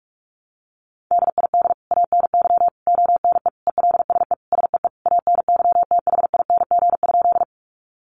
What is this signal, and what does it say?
Morse code, heard as a keyed tone, about 46 wpm, 700 Hz. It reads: DID ANY ONE ELSE SEE ANYTHING?
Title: Sentences from Top 1000 Words 45wpm